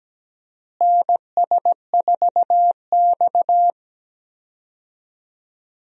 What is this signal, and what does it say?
Morse code, heard as a keyed tone, about 17 wpm, 700 Hz. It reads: NS4X
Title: USA Call Signs - 4 Characters - Repeated 3x - 17wpm